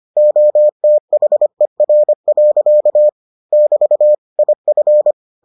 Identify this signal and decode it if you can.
Morse code, heard as a keyed tone, about 25 wpm, 600 Hz. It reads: OTHER. <BT> IF